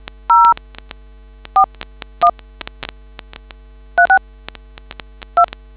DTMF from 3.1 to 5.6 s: 36 2